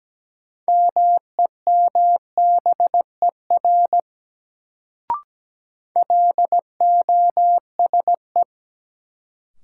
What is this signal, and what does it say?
Morse code, heard as a keyed tone, about 17 wpm, 700 Hz. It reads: MEMBER LOSE
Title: Sentences from Top 1100 Words - New Word Drill - 17wpm